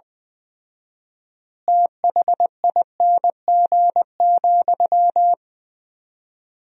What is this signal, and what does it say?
Morse code, heard as a keyed tone, about 20 wpm, 700 Hz. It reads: THING,
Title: 4x Word-Spacing - Sentences from Top 200 Words - 20wpm